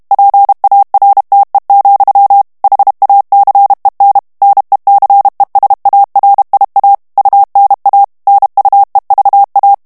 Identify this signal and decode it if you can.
Morse code, heard as a keyed tone, about 32 wpm, 800 Hz. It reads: PARTE, HACEN NECESARIA UNA NUEVA
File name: ESPAGNOL TEXTE EN CLAIR 2 VITESSE 30 MOTS MINUTES